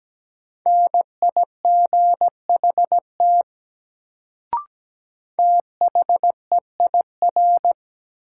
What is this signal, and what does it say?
Morse code, heard as a keyed tone, about 17 wpm, 700 Hz. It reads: NIGHT THEIR
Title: Sentences from Top 300 Words - New Word Drill - 17wpm